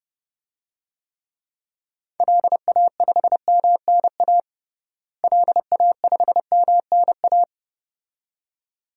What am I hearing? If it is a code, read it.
Morse code, heard as a keyed tone, about 30 wpm, 700 Hz. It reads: LA5MNA LA5MNA
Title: DXCC-100 Call Signs - 6 Characters - Repeated 2x - 30wpm